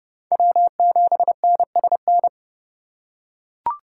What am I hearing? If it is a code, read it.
Morse code, heard as a keyed tone, about 30 wpm, 700 Hz. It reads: W7NSN
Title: USA Call Signs - 4x Character Spacing - 30wpm